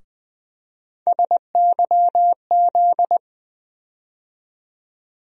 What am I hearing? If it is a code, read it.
Morse code, heard as a keyed tone, about 20 wpm, 700 Hz. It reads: SYZ
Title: USA Call Sign Postfixes - 3 Letters 20wpm